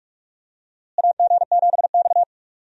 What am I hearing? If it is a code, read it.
Morse code, heard as a keyed tone, about 43 wpm, 700 Hz. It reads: AG7X
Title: USA Call Signs - 4 Characters - Repeated 3x - 45wpm